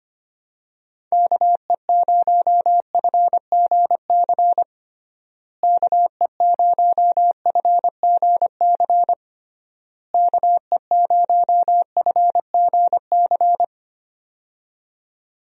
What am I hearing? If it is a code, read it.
Morse code, heard as a keyed tone, about 25 wpm, 700 Hz. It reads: KE0FGC KE0FGC KE0FGC
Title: USA Call Signs - 6 Characters - Repeated 3x - 25wpm